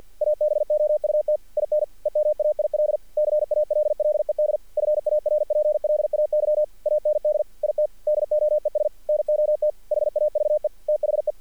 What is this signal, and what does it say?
Morse code, heard as a keyed tone, about 48 wpm, 600 Hz. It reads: ABOUT IN EMAIL CALLED FARPLAY AND IT DOES NOT HAVE THE
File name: archived / CRSnet